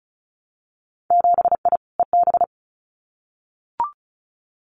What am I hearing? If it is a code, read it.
Morse code, heard as a keyed tone, about 36 wpm, 700 Hz. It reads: 7I EB
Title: Winter Field Day Exchanges Encore 35wpm